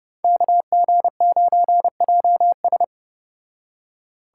Morse code, heard as a keyed tone, about 34 wpm, 700 Hz. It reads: KG9JS